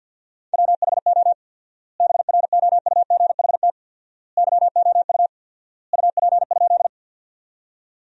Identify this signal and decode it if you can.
Morse code, heard as a keyed tone, about 51 wpm, 700 Hz. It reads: WHO BROUGHT YOU UP?
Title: 4x Word-Spacing - Sentences from Top 500 Words - 50wpm